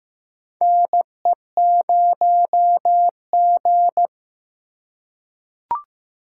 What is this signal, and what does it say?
Morse code, heard as a keyed tone, about 15 wpm, 700 Hz. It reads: NE0G
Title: USA Call Signs - 4 Characters - 4x Character Spacing - 15wpm